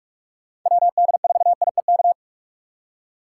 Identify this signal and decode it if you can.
Morse code, heard as a keyed tone, about 43 wpm, 700 Hz. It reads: WD4IEK
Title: USA Call Signs - 6 Characters - 4x Character Spacing - 45wpm